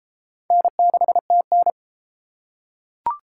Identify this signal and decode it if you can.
Morse code, heard as a keyed tone, about 28 wpm, 700 Hz. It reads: N6TN